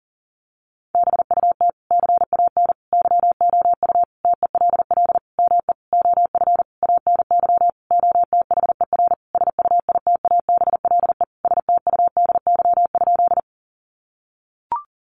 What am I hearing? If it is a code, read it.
Morse code, heard as a keyed tone, about 40 wpm, 700 Hz. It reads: BUT CAN YOU TELL ME OF ANY OTHER SUITABLE STUDY?
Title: Sentences from Top 1400 Words 40wpm